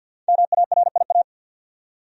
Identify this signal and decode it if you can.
Morse code, heard as a keyed tone, about 52 wpm, 700 Hz. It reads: MARIA
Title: USA Common Names - 50wpm